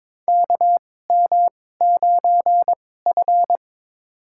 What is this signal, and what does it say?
Morse code, heard as a keyed tone, about 22 wpm, spaced as for 16 wpm, 700 Hz. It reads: KM9F
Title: USA Call Signs - 4 Characters - 2x Character Spacing - 22wpm